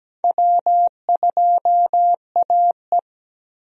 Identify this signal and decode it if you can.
Morse code, heard as a keyed tone, about 17 wpm, 700 Hz. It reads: W2AE
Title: USA Call Signs - 4 Characters - Repeated 2x - 17wpm